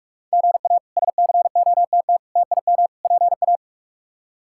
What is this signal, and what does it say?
Morse code, heard as a keyed tone, about 46 wpm, 700 Hz. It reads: GA SCOTT TIM PA
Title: K1USN SST QSO Practice - 45wpm